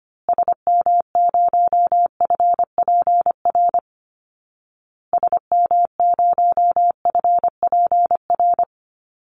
Morse code, heard as a keyed tone, about 25 wpm, 700 Hz. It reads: SM0FPR SM0FPR
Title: Call Signs Repeated 2x - 25wpm